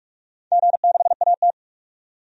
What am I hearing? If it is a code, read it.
Morse code, heard as a keyed tone, about 45 wpm, 700 Hz. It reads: G6AT
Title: DXCC-100 Call Signs - 4 Characters - 45wpm